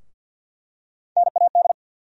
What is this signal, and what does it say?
Morse code, heard as a keyed tone, about 50 wpm, 700 Hz. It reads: NAD
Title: USA Call Sign Postfixes - 3 Letters 50wpm